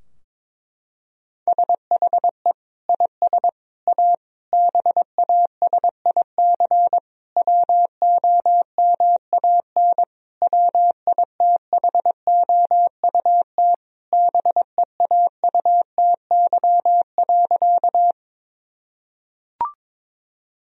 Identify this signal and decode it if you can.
Morse code, heard as a keyed tone, about 22 wpm, 700 Hz. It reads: SHE IS A BASIC WOMAN WITHOUT BEAUTY.
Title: Sentences from Top 1000 Words - Review Part 3 - 22wpm